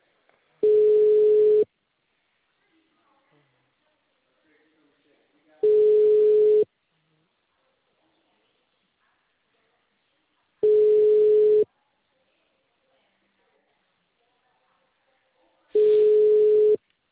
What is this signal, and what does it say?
Ringback tone (Europe, 425 Hz), with speech.
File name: out- -2007- - - .11569